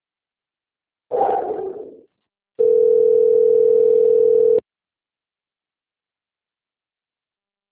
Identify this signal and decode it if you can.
Ringback tone (North America), 440 + 480 Hz.
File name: out- - - - - .65150